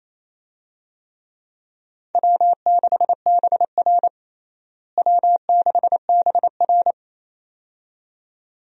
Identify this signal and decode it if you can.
Morse code, heard as a keyed tone, about 28 wpm, 700 Hz. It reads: W6BR W6BR
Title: USA Call Signs - 4 Characters - Repeated 2x - 28wpm